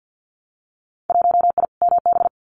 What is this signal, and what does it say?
Morse code, heard as a keyed tone, about 50 wpm, 700 Hz. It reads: 1I MB
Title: Winter Field Day Exchanges Encore 50wpm